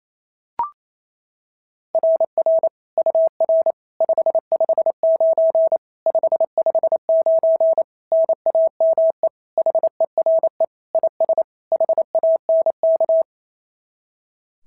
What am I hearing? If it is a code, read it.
Morse code, heard as a keyed tone, about 25 wpm, 650 Hz. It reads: RR UR 559 559 NAME HERE IS HANK
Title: SOTA QSO Practice 28wpm